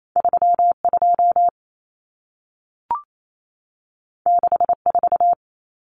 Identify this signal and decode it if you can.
Morse code, heard as a keyed tone, about 28 wpm, 700 Hz. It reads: 32 64
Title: Numbers - 2-Digit 28wpm